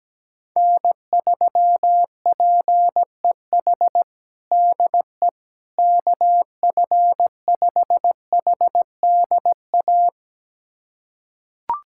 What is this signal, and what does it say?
Morse code, heard as a keyed tone, about 17 wpm, 700 Hz. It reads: N3PEH DE KF5HDA
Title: USA Callsign DE Callsign 17wpm